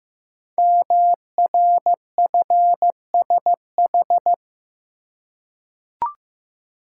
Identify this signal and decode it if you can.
Morse code, heard as a keyed tone, about 15 wpm, 700 Hz. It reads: MRFSH
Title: Code Groups - 5 Letters - 15wpm